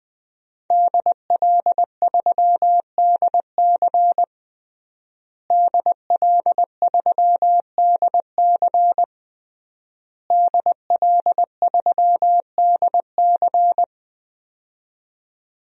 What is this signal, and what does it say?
Morse code, heard as a keyed tone, about 20 wpm, 700 Hz. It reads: DL3DC DL3DC DL3DC
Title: DXCC-100 Call Signs - 5 Characters - Repeated 3x - 20wpm